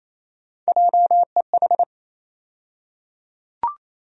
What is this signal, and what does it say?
Morse code, heard as a keyed tone, about 28 wpm, 700 Hz. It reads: JEH
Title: USA Call Sign Postfixes - 3 Letters 28wpm